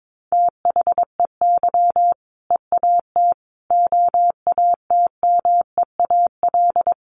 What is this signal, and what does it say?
Morse code, heard as a keyed tone, about 22 wpm, 700 Hz. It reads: THEY EAT OATMEAL
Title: Easy Phrases 22wpm